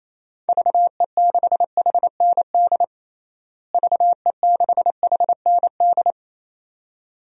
Morse code, heard as a keyed tone, about 28 wpm, 700 Hz. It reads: VE6HND VE6HND
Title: DXCC-100 Call Signs - 6 Characters - Repeated 2x - 28wpm